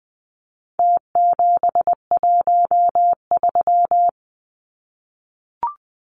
Morse code, heard as a keyed tone, about 20 wpm, 700 Hz. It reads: T713
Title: Numbers - Contest Style - 4-Digit 20wpm